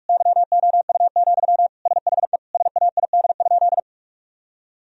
Morse code, heard as a keyed tone, about 41 wpm, 700 Hz. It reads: YOU, SHE SAID?
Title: Sentences from Top 1000 Words - Review Part 1 - 45wpm